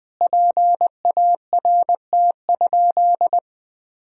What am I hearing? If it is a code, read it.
Morse code, heard as a keyed tone, about 21 wpm, 700 Hz. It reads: PART?
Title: Sentences from Top 1000 Words - Review Part 4 - 20wpm